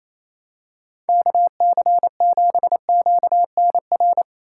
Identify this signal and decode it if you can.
Morse code, heard as a keyed tone, about 28 wpm, 700 Hz. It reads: KC7QNR
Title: USA Call Signs - 6 Characters - 4x Character Spacing - 28wpm